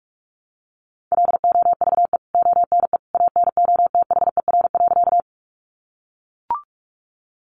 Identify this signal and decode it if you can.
Morse code, heard as a keyed tone, about 44 wpm, 700 Hz. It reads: LOVE ONE ANOTHER.
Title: Sentences from Top 600 Words 45wpm